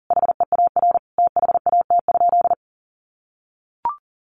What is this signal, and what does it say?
Morse code, heard as a keyed tone, about 40 wpm, 700 Hz. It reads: HEAR THAT?
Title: Sentences from Top 300 Words 40wpm